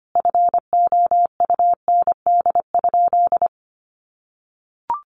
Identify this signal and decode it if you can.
Morse code, heard as a keyed tone, about 25 wpm, 700 Hz. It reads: FOUND?
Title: Sentences from Top 1400 Words 25wpm